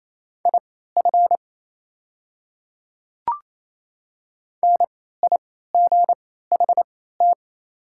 Morse code, heard as a keyed tone, about 27 wpm, spaced as for 16 wpm, 700 Hz. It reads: IF NIGHT
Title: 3x Character Spacing - Top 300 Words - 28wpm